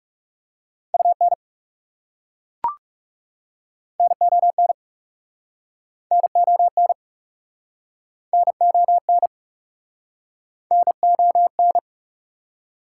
Morse code, heard as a keyed tone, about 41 wpm, 700 Hz. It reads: UN NON NON NON NON